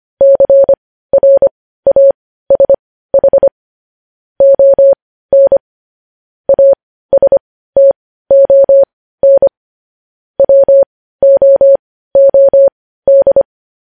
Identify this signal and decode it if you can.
Morse code, heard as a keyed tone, about 25 wpm, spaced as for 15 wpm, 550 Hz. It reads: CRASH ON ASTON WOOD
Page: News Headlines in Morse Code at 15 WPM